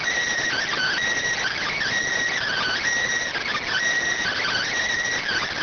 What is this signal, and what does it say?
Emergency Alert System (SAME) tones, heard as harsh buzzing bursts. It is heard from 0.5 to 4.7 s.